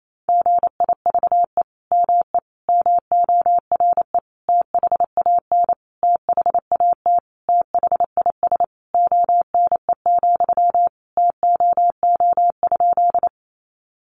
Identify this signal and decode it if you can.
Morse code, heard as a keyed tone, about 28 wpm, 700 Hz. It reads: GIVE ME MORE THAN THAT THIS ONE, TOO?
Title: Sentences from Top 200 Words 28wpm